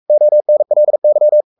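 Morse code, heard as a keyed tone, about 44 wpm, 600 Hz. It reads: ONLY
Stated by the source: Morse code at 43 wpm sending Ken Follett's novel Never in English, from CW med Gnister.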